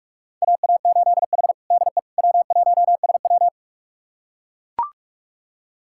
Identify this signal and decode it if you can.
Morse code, heard as a keyed tone, about 42 wpm, 700 Hz. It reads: AA8H DE W1SW
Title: USA Callsign DE Callsign - 4 Characters - 45wpm